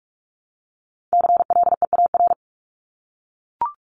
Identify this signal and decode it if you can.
Morse code, heard as a keyed tone, about 44 wpm, 700 Hz. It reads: CLEAR
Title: LY Suffix - Top 100 Words with Base Words 45wpm